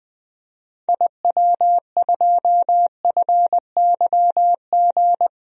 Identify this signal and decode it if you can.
Morse code, heard as a keyed tone, about 20 wpm, 700 Hz. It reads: IW2FYG
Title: DXCC-100 Call Signs - 6 Characters - Repeated 3x - 20wpm